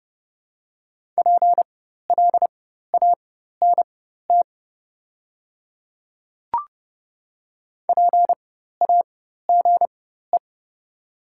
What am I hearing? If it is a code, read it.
Morse code, heard as a keyed tone, about 30 wpm, spaced as for 14 wpm, 700 Hz. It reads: PLANT PAGE
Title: 4x Character Spacing - Top 300 Words - 30wpm